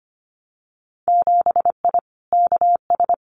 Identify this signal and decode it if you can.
Morse code, heard as a keyed tone, about 25 wpm, 700 Hz. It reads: 7I KS
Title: Winter Field Day Exchanges Encore 25wpm